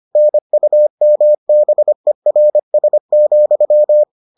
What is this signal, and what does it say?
Morse code, heard as a keyed tone, about 25 wpm, 600 Hz. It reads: NUMBERS,